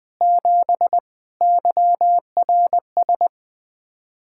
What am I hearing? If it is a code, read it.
Morse code, heard as a keyed tone, about 20 wpm, 700 Hz. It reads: 7 YRS
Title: Practice QSO - Exchange 3 - More Info - 20wpm